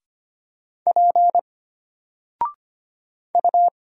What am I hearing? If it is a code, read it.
Morse code, heard as a keyed tone, about 25 wpm, 700 Hz. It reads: P U
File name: Single-Letter-Number-25wpm